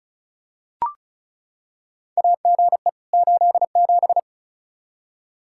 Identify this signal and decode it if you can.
Morse code, heard as a keyed tone, about 34 wpm, 700 Hz. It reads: AGE 87
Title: K1USN SST QSO Practice 35wpm